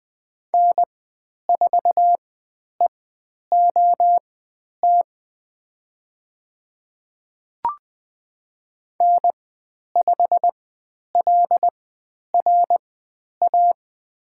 Morse code, heard as a keyed tone, about 20 wpm, spaced as for 10 wpm, 700 Hz. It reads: N4EOT N5LRA
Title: 102 Lesson 17g - Letter u - Call Signs 20wpm